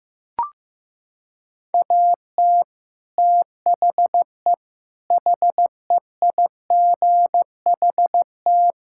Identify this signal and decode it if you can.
Morse code, heard as a keyed tone, about 16 wpm, 700 Hz. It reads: AT THE HEIGHT
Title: Sentences from Top 1600 Words 15wpm